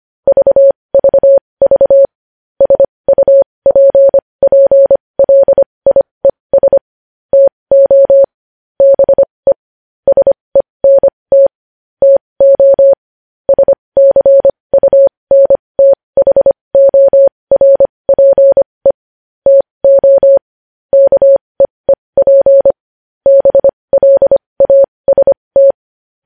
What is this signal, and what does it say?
Morse code, heard as a keyed tone, about 25 wpm, spaced as for 20 wpm, 550 Hz. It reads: VVV SUPPLIES TO BE SENT TO SCUNTHORPE TO KEEP BLAST
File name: podcast / News headlines-1x-25x20